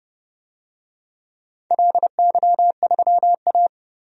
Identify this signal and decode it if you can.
Morse code, heard as a keyed tone, about 31 wpm, 700 Hz. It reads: LY3A
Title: DXCC-100 Call Signs - 4 Characters - 30wpm